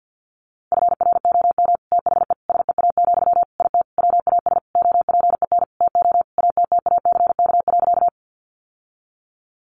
Morse code, heard as a keyed tone, about 50 wpm, 700 Hz. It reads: FROM THE SEA, IT WAS OPEN TO ATTACK.
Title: Sentences from Top 1500 Words 50wpm